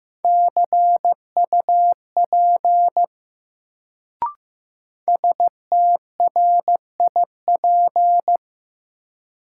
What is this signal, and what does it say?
Morse code, heard as a keyed tone, about 15 wpm, 700 Hz. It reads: CUP STRIP
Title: Sentences from Top 1600 Words - New Word Drill - 15wpm